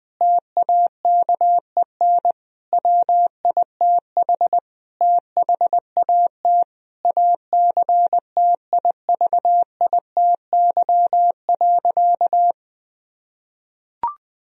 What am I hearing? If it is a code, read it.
Morse code, heard as a keyed tone, about 19 wpm, 700 Hz. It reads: TAKEN WITH THAT ACTIVITY.
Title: Sentences from Top 1000 Words - Review Part 4 - 20wpm